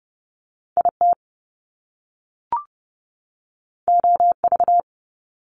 Morse code, heard as a keyed tone, about 30 wpm, 700 Hz. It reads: IT OV
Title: Top Two Letter Combinations in Words 30wpm